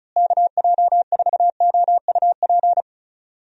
Morse code, heard as a keyed tone, about 34 wpm, 700 Hz. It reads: KJ4OUP
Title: USA Call Signs - 6 Characters - Repeated 3x - 35wpm